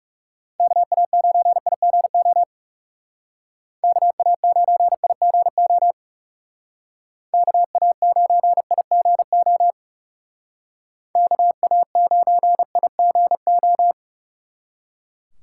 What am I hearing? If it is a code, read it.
Morse code, heard as a keyed tone, about 36 wpm, 700 Hz. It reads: KA9IGO KA9IGO KA9IGO KA9IGO
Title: Speed Racing - USA 2x3 Call Signs 30wpm